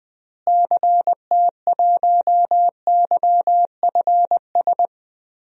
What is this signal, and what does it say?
Morse code, heard as a keyed tone, about 20 wpm, 700 Hz. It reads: CT1YFS
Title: DXCC-100 Call Signs - 6 Characters - Repeated 3x - 20wpm